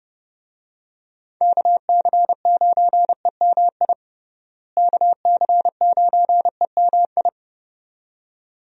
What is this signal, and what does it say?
Morse code, heard as a keyed tone, about 30 wpm, 700 Hz. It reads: KC9EMI KC9EMI
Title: USA Call Signs - 6 Characters - Repeated 2x - 30wpm